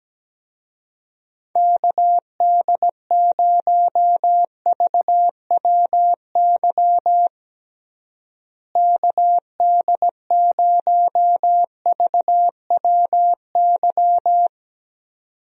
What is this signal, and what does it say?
Morse code, heard as a keyed tone, about 17 wpm, 700 Hz. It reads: KD0VWY KD0VWY
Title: USA Call Signs - 6 Characters - Repeated 2x - 17wpm